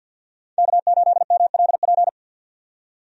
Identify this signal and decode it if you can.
Morse code, heard as a keyed tone, about 53 wpm, 700 Hz. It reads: K8MLP
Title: USA Call Signs - 5 Characters - 50wpm